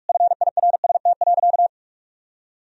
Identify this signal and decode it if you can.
Morse code, heard as a keyed tone, about 46 wpm, 700 Hz. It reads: FIRST.
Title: Sentences from Top 1000 Words - Review Part 1 - 45wpm